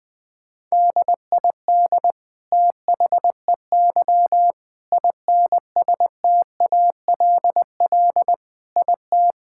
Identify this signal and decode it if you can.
Morse code, heard as a keyed tone, about 21 wpm, 700 Hz. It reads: DID THEY INSTALL IT
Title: Sentences from Top 1600 Words 20wpm